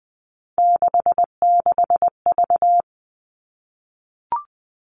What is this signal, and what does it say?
Morse code, heard as a keyed tone, about 19 wpm, 700 Hz. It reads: 66V
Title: Three Challenging Letter-Number Combinations 20wpm